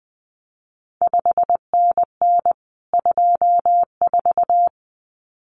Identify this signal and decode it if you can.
Morse code, heard as a keyed tone, about 20 wpm, 700 Hz. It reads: 5NN 24
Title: CQ WW DX Contest Exchange Encore - 20wpm